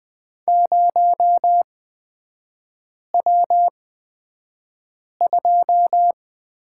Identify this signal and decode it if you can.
Morse code, heard as a keyed tone, about 20 wpm, spaced as for 5 wpm, 690 Hz. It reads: 0W2
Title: 229 Lesson 37e - Letter x - Three Characters 20wpm